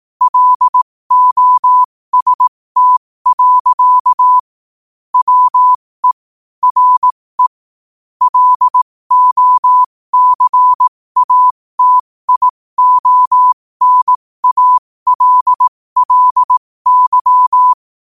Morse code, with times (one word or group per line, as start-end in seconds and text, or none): 0.2-4.4 LOST.
5.1-6.1 WE
6.6-7.5 RE
8.2-17.7 LOCATIONALLY